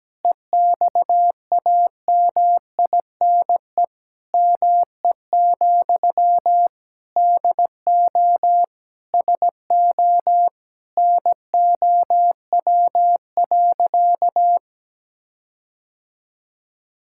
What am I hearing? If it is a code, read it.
Morse code, heard as a keyed tone, about 17 wpm, 700 Hz. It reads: EXAMINE ME, DO SO NOW.
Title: Sentences from Top 1600 Words 17wpm